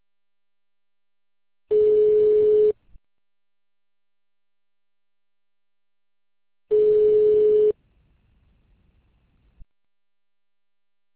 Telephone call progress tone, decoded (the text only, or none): ringback tone